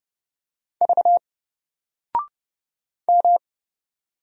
Morse code, heard as a keyed tone, about 30 wpm, 700 Hz. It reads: V M
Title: Single Character (Letter:Number:Punctuation:Prosign) 30wpm